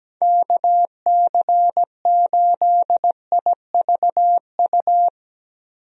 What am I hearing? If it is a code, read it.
Morse code, heard as a keyed tone, about 17 wpm, 700 Hz. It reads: KC8IVU
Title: USA Call Signs - 6 Characters - Repeated 3x - 17wpm